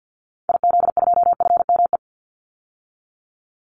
Morse code, heard as a keyed tone, about 50 wpm, 700 Hz. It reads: IZ2FME
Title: Call Signs - 50wpm